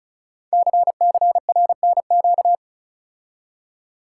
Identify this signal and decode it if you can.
Morse code, heard as a keyed tone, about 36 wpm, 700 Hz. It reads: CCRNQ
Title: Code Groups - 5 Letter:Numbers - 35wpm